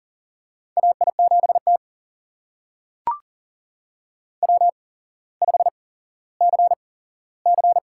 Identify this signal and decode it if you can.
Morse code, heard as a keyed tone, about 40 wpm, spaced as for 17 wpm, 700 Hz. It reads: AI7T W5CC
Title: USA Call Signs - 4 Characters - 8x Character Spacing - 40wpm